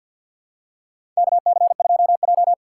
Morse code, heard as a keyed tone, about 47 wpm, 700 Hz. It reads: KC2J
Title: USA Call Signs - 4 Characters - 3x Character Spacing - 50wpm